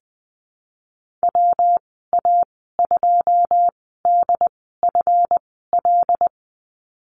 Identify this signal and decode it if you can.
Morse code, heard as a keyed tone, about 20 wpm, spaced as for 15 wpm, 700 Hz. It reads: WA2DFL
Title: 186 Lesson 30g - Letter b - Call Signs 20wpm